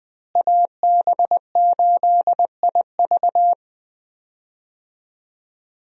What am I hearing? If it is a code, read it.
Morse code, heard as a keyed tone, about 20 wpm, 700 Hz. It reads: AB8IV
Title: USA Call Signs - 5 Characters - Repeated 3x - 20wpm